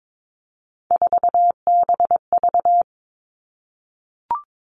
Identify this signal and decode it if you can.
Morse code, heard as a keyed tone, about 22 wpm, 700 Hz. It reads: 4BV
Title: Three Challenging Letter-Number Combinations 22wpm